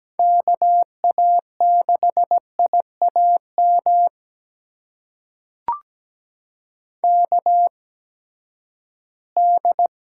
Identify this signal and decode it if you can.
Morse code, heard as a keyed tone, about 17 wpm, spaced as for 12 wpm, 700 Hz. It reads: KA6IAM KD
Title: USA Call Signs - 6 Characters - 8x Character Spacing - 17wpm